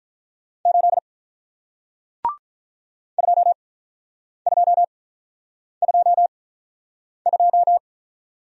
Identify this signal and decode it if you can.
Morse code, heard as a keyed tone, about 48 wpm, spaced as for 9 wpm, 700 Hz. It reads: 8 2222